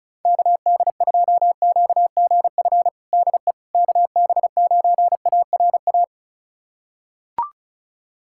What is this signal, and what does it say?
Morse code, heard as a keyed tone, about 34 wpm, 690 Hz. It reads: KD2QGF DE KB9ARA
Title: USA Callsign DE Callsign - 6 Characters - 35wpm